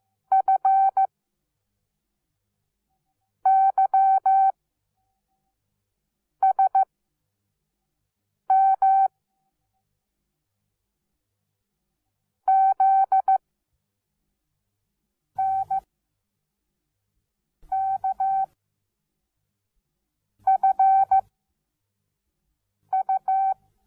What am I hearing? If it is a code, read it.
Morse code, heard as a keyed tone, about 15 wpm, 770 Hz. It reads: F Y S M Z N K F U
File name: Onlinelehrgang / morsen / DJ4UF Morsen17